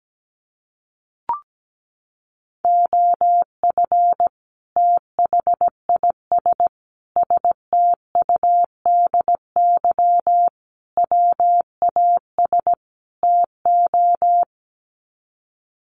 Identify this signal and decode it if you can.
Morse code, heard as a keyed tone, about 17 wpm, 700 Hz. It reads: OF THIS STUDY WAS TO
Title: Sets of 5 Words 17wpm